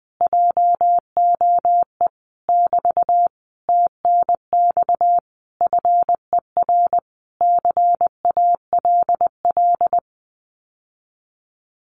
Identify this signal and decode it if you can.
Morse code, heard as a keyed tone, about 20 wpm, 700 Hz. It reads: JOE <BT> TNX FER CALL
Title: Common QSO Phrases Encore 20wpm